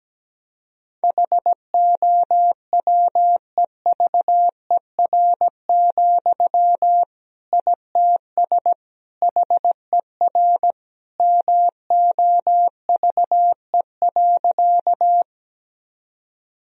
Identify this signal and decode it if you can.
Morse code, heard as a keyed tone, about 17 wpm, 700 Hz. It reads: HOWEVER, ITS HER MOVE.
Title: Sentences from Top 1000 Words - Review Part 1 - 17wpm